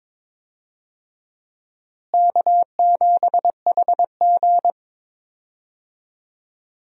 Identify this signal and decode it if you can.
Morse code, heard as a keyed tone, about 22 wpm, 700 Hz. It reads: K7HG
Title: USA Call Signs - 4 Characters - Repeated 2x - 22wpm